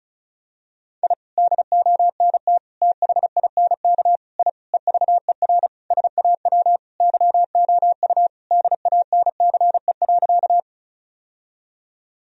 Morse code, heard as a keyed tone, about 34 wpm, 700 Hz. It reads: I DONT THINK I EVER SAW YOU DANCE.
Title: Sentences from Top 1000 Words - Review Part 4 - 35wpm